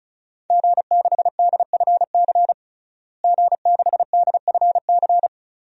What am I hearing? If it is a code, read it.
Morse code, heard as a keyed tone, about 36 wpm, 700 Hz. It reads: G6DFC G6DFC
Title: DXCC-100 Call Signs - 5 Characters - Repeated 3x - 35wpm